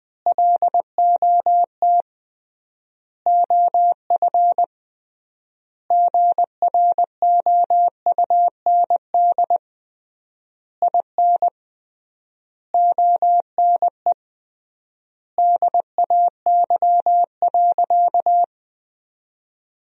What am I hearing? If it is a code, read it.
Morse code, heard as a keyed tone, about 21 wpm, 700 Hz. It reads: LOT OF GROUND IN ONE DAY.
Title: 3x Word-Spacing - Sentences from Top 500 Words - 20wpm